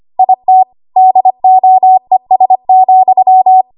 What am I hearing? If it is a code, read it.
Morse code, heard as a keyed tone, about 25 wpm, 750 Hz. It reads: IT DOES,
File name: Archive / 25 WPM / 25WPM